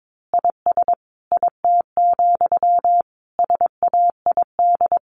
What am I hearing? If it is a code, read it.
Morse code, heard as a keyed tone, about 22 wpm, 700 Hz. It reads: IS IT, SAID
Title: A Sound Of Thunder 22wpm